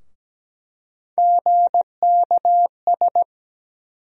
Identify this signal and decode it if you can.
Morse code, heard as a keyed tone, about 17 wpm, 700 Hz. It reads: GKS
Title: USA Call Sign Postfixes - 3 Letters 17wpm